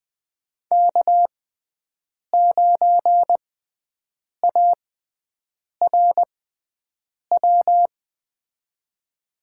Morse code, heard as a keyed tone, about 21 wpm, spaced as for 7 wpm, 700 Hz. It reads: K9ARW
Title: USA Call Signs - 5 Characters - 6x Character Spacing - 20wpm